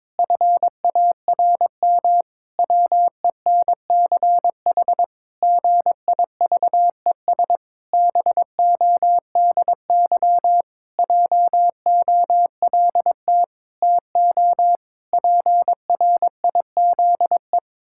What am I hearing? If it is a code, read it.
Morse code, heard as a keyed tone, about 22 wpm, 700 Hz. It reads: FARM WENCH GIVES BODY JOLT TO PRIZE
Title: Pangram Challenge v2 - 22wpm